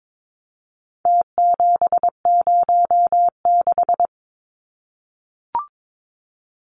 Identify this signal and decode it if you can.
Morse code, heard as a keyed tone, about 22 wpm, 700 Hz. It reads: T706
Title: Numbers - Contest Style - 4-Digit 22wpm